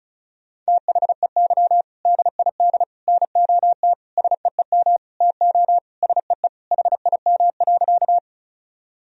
Morse code, heard as a keyed tone, about 36 wpm, 700 Hz. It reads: THEY DID NOT SEEM TO SEE HIM.
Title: Sentences from Top 1000 Words - Review Part 1 - 35wpm